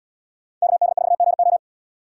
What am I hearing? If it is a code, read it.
Morse code, heard as a keyed tone, about 77 wpm, 700 Hz. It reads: KD4GO